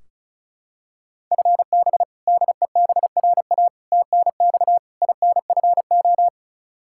Morse code, heard as a keyed tone, about 34 wpm, 700 Hz. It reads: FB DEBRA TNX INFO
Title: Practice QSO - Exchange 3 - More Info - 35wpm